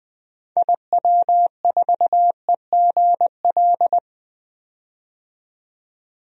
Morse code, heard as a keyed tone, about 20 wpm, 700 Hz. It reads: IW4EGL
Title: DXCC-100 Call Signs - 6 Characters - 20wpm